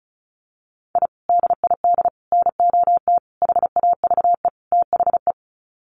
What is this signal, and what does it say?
Morse code, heard as a keyed tone, about 34 wpm, 700 Hz. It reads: I DID NOT HAVE THE
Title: Sets of 5 Words Encore 35wpm